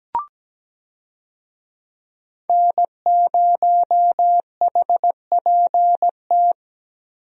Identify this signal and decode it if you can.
Morse code, heard as a keyed tone, about 17 wpm, 700 Hz. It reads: N0HPT
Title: USA Call Signs - 5 Characters - Repeated 3x - 17wpm